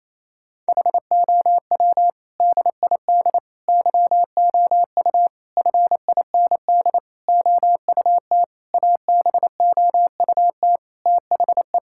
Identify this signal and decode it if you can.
Morse code, heard as a keyed tone, about 28 wpm, 700 Hz. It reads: HOW DID YOU FIND OUT ABOUT THE
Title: Sentences from Top 1000 Words - Review Part 5 - 28wpm